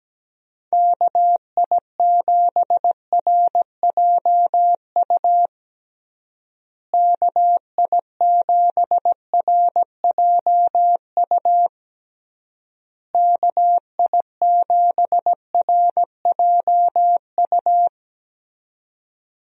Morse code, heard as a keyed tone, about 17 wpm, 700 Hz. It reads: KI7RJU KI7RJU KI7RJU
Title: USA Call Signs - 6 Characters - Repeated 3x - 17wpm